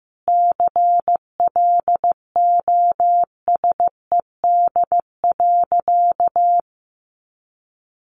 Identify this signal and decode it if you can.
Morse code, heard as a keyed tone, about 15 wpm, 700 Hz. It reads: CLOSED.
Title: Sentences from Top 1400 Words 15wpm